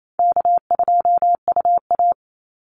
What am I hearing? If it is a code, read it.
Morse code, heard as a keyed tone, about 28 wpm, 700 Hz. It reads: K2UA
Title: Easy Call Signs Repeated 2x - 28wpm